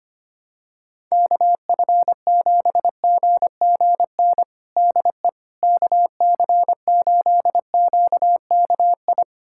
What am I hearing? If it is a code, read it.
Morse code, heard as a keyed tone, about 25 wpm, 700 Hz. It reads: KF7GGN DE KC8QKI
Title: USA Callsign DE Callsign 25wpm